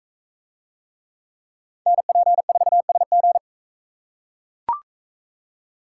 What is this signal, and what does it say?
Morse code, heard as a keyed tone, about 42 wpm, 700 Hz. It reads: NP4SG